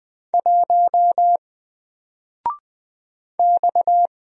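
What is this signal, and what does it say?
Morse code, heard as a keyed tone, about 20 wpm, 700 Hz. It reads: 1 X
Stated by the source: Single letter and number Morse drill at 20 wpm, from N4LCD.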